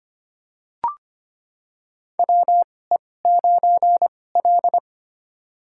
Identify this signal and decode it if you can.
Morse code, heard as a keyed tone, about 25 wpm, spaced as for 18 wpm, 700 Hz. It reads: WE9L
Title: USA Call Signs - 4 Characters - 2x Character Spacing - 25wpm